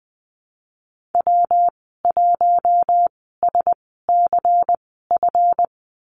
Morse code, heard as a keyed tone, about 20 wpm, spaced as for 14 wpm, 700 Hz. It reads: W1SCF
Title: 154 Lesson 25g - Letter p - Call Signs 20wpm